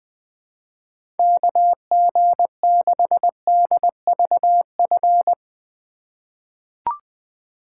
Morse code, heard as a keyed tone, about 20 wpm, 700 Hz. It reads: KG6DVF
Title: USA Call Signs - 6 Characters - 4x Character Spacing - 20wpm